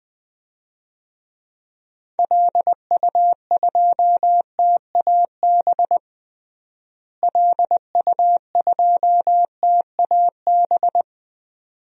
Morse code, heard as a keyed tone, about 20 wpm, 700 Hz. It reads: LU2TAB LU2TAB
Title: DXCC-100 Call Signs - 6 Characters - Repeated 2x - 20wpm